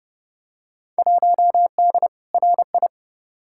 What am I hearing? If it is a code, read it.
Morse code, heard as a keyed tone, about 32 wpm, 700 Hz. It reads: 1D RI
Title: Field Day Exchanges - 30wpm